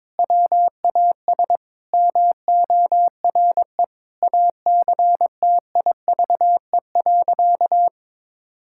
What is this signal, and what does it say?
Morse code, heard as a keyed tone, about 22 wpm, 700 Hz. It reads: WAS MORE ACTIVE.
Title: Sentences from Top 1000 Words - Review Part 6 - 22wpm